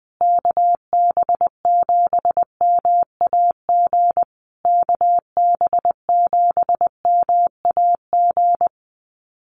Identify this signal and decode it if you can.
Morse code, heard as a keyed tone, about 19 wpm, 700 Hz. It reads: KB7MAG KB7MAG
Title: USA Call Signs Repeated 3x 20wpm